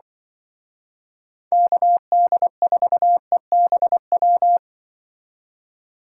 Morse code, heard as a keyed tone, about 25 wpm, 700 Hz. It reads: KD4EBW
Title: Speed Racing - USA 2x3 Call Signs 20wpm